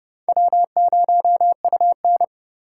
Morse code, heard as a keyed tone, about 30 wpm, 700 Hz. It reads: W0UN
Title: USA Call Signs - 4 Characters - 4x Character Spacing - 30wpm